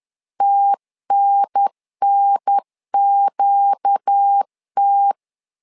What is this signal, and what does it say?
Morse code, heard as a keyed tone, about 10 wpm, 800 Hz. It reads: TNNQT
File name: LEZIONI / 5 LBVC / 5 L 10wpm 2259